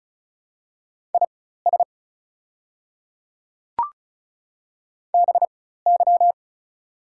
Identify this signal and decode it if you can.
Morse code, heard as a keyed tone, about 34 wpm, spaced as for 16 wpm, 700 Hz. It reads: IS BY
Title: 4x Character Spacing - Top 100 Words - 35wpm